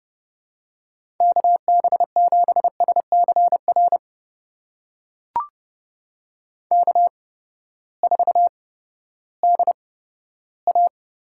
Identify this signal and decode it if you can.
Morse code, heard as a keyed tone, about 30 wpm, spaced as for 15 wpm, 700 Hz. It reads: KB7SCR K4DA
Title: USA Call Signs - 8x Character Spacing - 30wpm